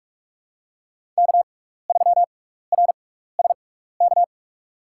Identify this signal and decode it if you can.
Morse code, heard as a keyed tone, about 41 wpm, spaced as for 15 wpm, 700 Hz. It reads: K3RSK
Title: USA Call Signs - 5 Characters - 6x Character Spacing - 45wpm